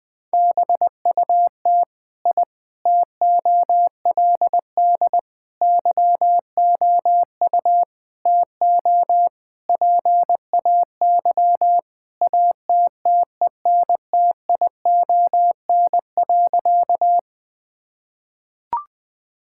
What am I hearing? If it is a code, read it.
Morse code, heard as a keyed tone, about 21 wpm, 700 Hz. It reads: BUT I TOLD YOU TO PAY ATTENTION.
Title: Sentences from Top 1000 Words - Review Part 6 - 20wpm